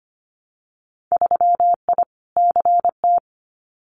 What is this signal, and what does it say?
Morse code, heard as a keyed tone, about 25 wpm, 690 Hz. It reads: 3I CT
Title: Winter Field Day Exchanges 25wpm